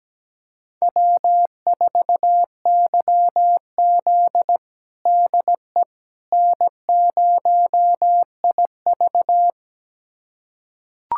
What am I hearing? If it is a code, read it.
Morse code, heard as a keyed tone, about 17 wpm, 700 Hz. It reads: W4YZ DE N0IV
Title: USA Callsign DE Callsign - 4 Characters - 17wpm